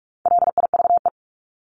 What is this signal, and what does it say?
Morse code, heard as a keyed tone, about 44 wpm, 700 Hz. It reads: LIVE
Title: ED Suffix - Top 100 Words with Base Words 45wpm